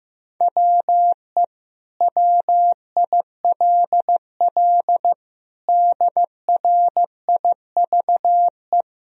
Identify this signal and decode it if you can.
Morse code, heard as a keyed tone, about 15 wpm, 700 Hz. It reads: WE WILL DRIVE
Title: Sentences from Top 1000 Words - Review Part 2 - 15wpm